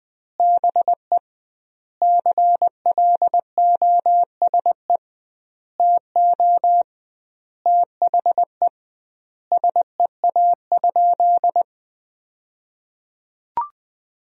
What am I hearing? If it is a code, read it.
Morse code, heard as a keyed tone, about 20 wpm, 700 Hz. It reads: BE CLOSE TO THE SEA?
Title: 2x Word-Spacing - Sentences from Top 300 Words - 20wpm